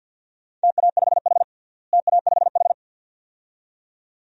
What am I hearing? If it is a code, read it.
Morse code, heard as a keyed tone, about 47 wpm, 700 Hz. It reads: TA5H TA5H
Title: DXCC-100 Call Signs - 4 Characters - Repeated 2x - 50wpm